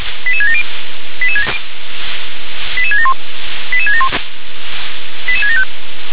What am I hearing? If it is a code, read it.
Five-tone selective call (ZVEI), heard as quick runs of five tones, tones of about 70 ms. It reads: 88670 88671 88671 88675